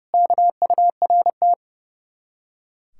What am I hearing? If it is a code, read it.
Morse code, heard as a keyed tone, about 38 wpm, 700 Hz. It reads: KURT